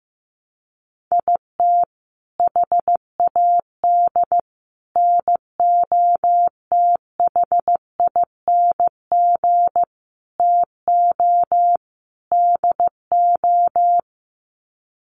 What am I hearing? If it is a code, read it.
Morse code, heard as a keyed tone, about 15 wpm, 700 Hz. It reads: IT HAD NOTHING TO DO
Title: Sets of 5 Words Encore2 15wpm